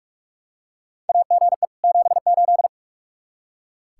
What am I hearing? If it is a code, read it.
Morse code, heard as a keyed tone, about 43 wpm, 700 Hz. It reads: AGE 78
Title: Practice QSO - Exchange 3 - More Info - 45wpm